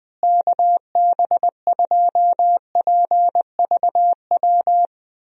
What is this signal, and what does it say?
Morse code, heard as a keyed tone, about 20 wpm, 700 Hz. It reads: KB2PVW
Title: USA Call Signs - 6 Characters - Repeated 2x - 20wpm